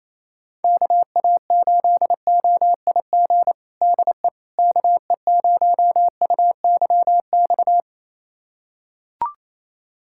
Morse code, heard as a keyed tone, about 28 wpm, 700 Hz. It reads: KA8OIG DE KE0UYX
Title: USA Callsign DE Callsign - 6 Characters - 28wpm